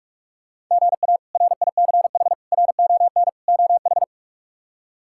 Morse code, heard as a keyed tone, about 48 wpm, 700 Hz. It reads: GA RICH RON OH
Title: K1USN SST QSO Practice - 45wpm